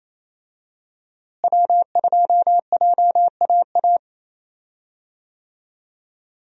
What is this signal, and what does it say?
Morse code, heard as a keyed tone, about 28 wpm, 700 Hz. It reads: W2JAA
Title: USA Call Signs - 5 Characters - Repeated 2x - 28wpm